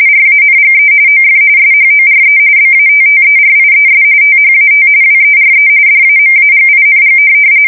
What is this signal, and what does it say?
RTTY, about 45.45 baud. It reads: RANGE OF SHORE BATTERIES THAT WERE DAMAGING
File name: fsk170 / USS-New-Jersey